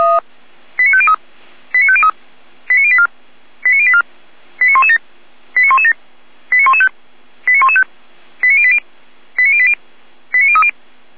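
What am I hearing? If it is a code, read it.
Five-tone selective call (ZVEI), heard as quick runs of five tones, tones of about 70 ms. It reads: 89503 89503 89084 89084 89117 89117 89115 89115 89080 89080 89030
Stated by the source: ZVEI five-tone selective call received by SDR.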